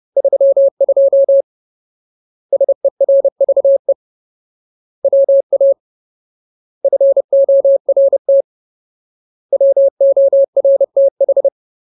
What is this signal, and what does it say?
Morse code, heard as a keyed tone, about 30 wpm, 550 Hz. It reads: 32 SERVE WA FORT WORTH
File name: Ultimate Word List / 30WPM Ultimate